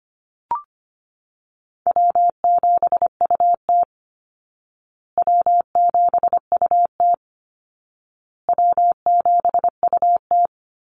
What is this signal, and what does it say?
Morse code, heard as a keyed tone, about 25 wpm, 700 Hz. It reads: W7UT W7UT W7UT
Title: Easy Call Signs Repeated 3x - 25wpm